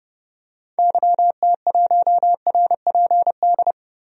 Morse code, heard as a keyed tone, about 30 wpm, 700 Hz. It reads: YT1RPD
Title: DXCC-100 Call Signs - 6 Characters - 30wpm